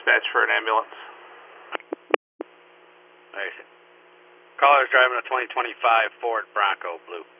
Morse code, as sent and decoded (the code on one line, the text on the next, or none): none